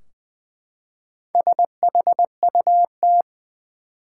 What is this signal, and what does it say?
Morse code, heard as a keyed tone, about 20 wpm, 700 Hz. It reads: SHUT
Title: Four Letter Words - 20wpm